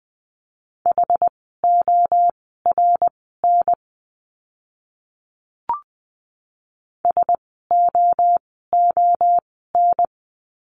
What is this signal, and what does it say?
Morse code, heard as a keyed tone, about 20 wpm, spaced as for 14 wpm, 700 Hz. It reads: HORN SOON
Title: 061 Lesson 11f - Letter h - Words 20wpm